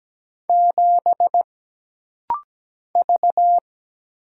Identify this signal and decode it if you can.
Morse code, heard as a keyed tone, about 17 wpm, 700 Hz. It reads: 7 V
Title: Single Letter-Number - Rapid-Fire 17wpm